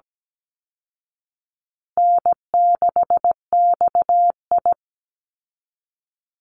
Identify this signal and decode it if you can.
Morse code, heard as a keyed tone, about 17 wpm, 700 Hz. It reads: N6XI
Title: Easy Call Signs Repeated 3x - 17wpm